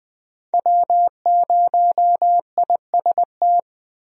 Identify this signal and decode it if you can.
Morse code, heard as a keyed tone, about 20 wpm, 700 Hz. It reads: W0IST
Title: USA Call Signs - 5 Characters - 20wpm